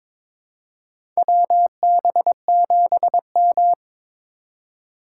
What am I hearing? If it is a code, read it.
Morse code, heard as a keyed tone, about 22 wpm, 700 Hz. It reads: WB7M
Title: USA Call Signs - 2x Character Spacing - 22wpm